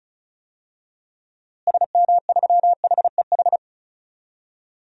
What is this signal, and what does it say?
Morse code, heard as a keyed tone, about 34 wpm, 700 Hz. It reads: SM3HEH
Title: DXCC-100 Call Signs - 6 Characters - Repeated 2x - 35wpm